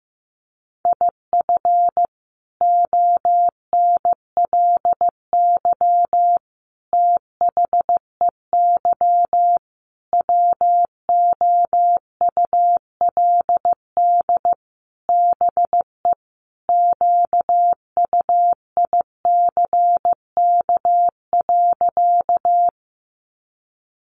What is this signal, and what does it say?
Morse code, heard as a keyed tone, about 15 wpm, 700 Hz. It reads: IF ONLY THEY WOULD BE QUICK.
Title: Sentences from Top 500 Words 15wpm